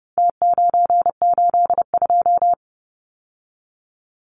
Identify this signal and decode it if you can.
Morse code, heard as a keyed tone, about 30 wpm, 700 Hz. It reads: T982
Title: Numbers - Contest Style - 4-Digit 30wpm